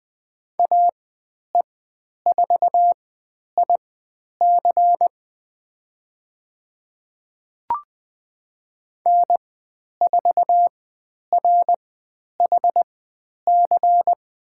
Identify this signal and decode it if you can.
Morse code, heard as a keyed tone, about 20 wpm, spaced as for 10 wpm, 700 Hz. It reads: AE4IC N4RHC
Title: 102 Lesson 17g - Letter u - Call Signs 20wpm